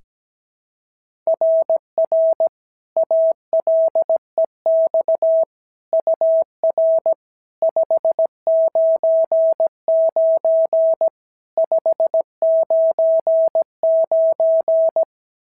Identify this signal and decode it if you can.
Morse code, heard as a keyed tone, about 17 wpm, 650 Hz. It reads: RR ALEX UR 599 599
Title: SOTA QSO Practice 17wpm